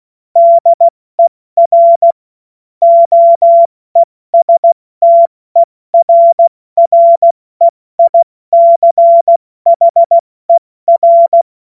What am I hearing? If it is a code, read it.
Morse code, heard as a keyed tone, about 16 wpm, spaced as for 14 wpm, 680 Hz. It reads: DER OESTERREICHER
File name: Morsekurs / Witze / Witz 30. Dezember 2025 70 F80